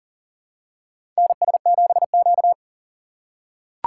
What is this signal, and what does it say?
Morse code, heard as a keyed tone, about 40 wpm, 700 Hz. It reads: NS7Q
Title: USA Call Signs - 4 Characters - 2x Character Spacing - 40wpm